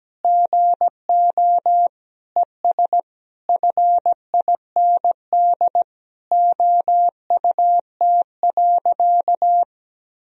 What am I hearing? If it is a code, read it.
Morse code, heard as a keyed tone, about 17 wpm, 700 Hz. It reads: GO ES FIND OUT.
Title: Sentences from Top 500 Words and Abbreviations 17wpm